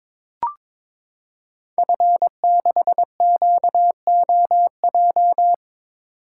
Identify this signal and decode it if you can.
Morse code, heard as a keyed tone, about 22 wpm, 700 Hz. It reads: F6QOJ
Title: Code Groups - 5 Letter:Numbers - 22wpm